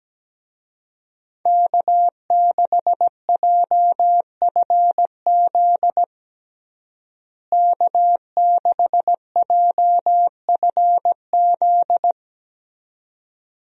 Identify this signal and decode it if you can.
Morse code, heard as a keyed tone, about 17 wpm, 700 Hz. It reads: K6JFZ K6JFZ
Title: USA Call Signs - 5 Characters - Repeated 2x - 17wpm